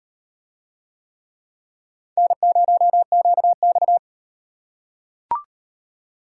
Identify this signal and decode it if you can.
Morse code, heard as a keyed tone, about 37 wpm, 700 Hz. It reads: N0QX